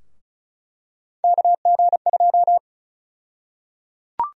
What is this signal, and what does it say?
Morse code, heard as a keyed tone, about 36 wpm, 700 Hz. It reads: KG2
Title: USA Call Sign Prefixes - Repeated 2x - 35wpm